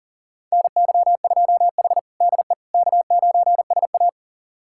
Morse code, heard as a keyed tone, about 40 wpm, 700 Hz. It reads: NY2H DE K9SA
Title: USA Callsign DE Callsign - 4 Characters - 40wpm